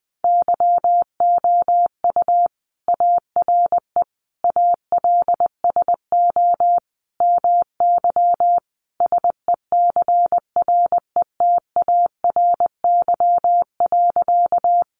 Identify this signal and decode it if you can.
Morse code, heard as a keyed tone, about 20 wpm, 700 Hz. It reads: YOU ARE ALSO MY SECRETARY.
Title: Sentences from Top 1300 Words 20wpm